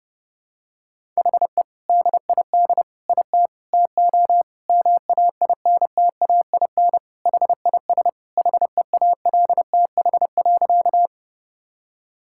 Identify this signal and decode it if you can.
Morse code, heard as a keyed tone, about 32 wpm, 700 Hz. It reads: HE DID IT TO MAINTAIN HIS HEALTH.
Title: Sentences from Top 1000 Words - Review Part 5 - 30wpm